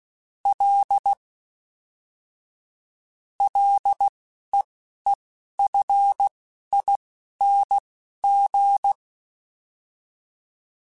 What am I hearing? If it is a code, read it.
Morse code, heard as a keyed tone, about 16 wpm, spaced as for 11 wpm, 800 Hz. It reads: L LEEFING
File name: F and L / F and L mp3 / F and L Prefixes 3x2 Twice @10